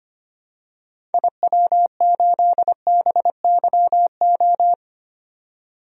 Morse code, heard as a keyed tone, about 25 wpm, 700 Hz. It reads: IW8BYO
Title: DXCC-100 Call Signs - 6 Characters - 25wpm